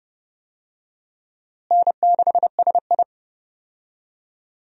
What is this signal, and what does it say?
Morse code, heard as a keyed tone, about 32 wpm, 700 Hz. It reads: N6SI
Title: USA Call Signs - 4 Characters - Repeated 3x - 30wpm